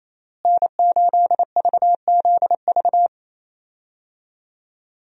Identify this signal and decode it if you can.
Morse code, heard as a keyed tone, about 28 wpm, 700 Hz. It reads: N8VZV
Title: USA Call Signs - 5 Characters - Repeated 3x - 28wpm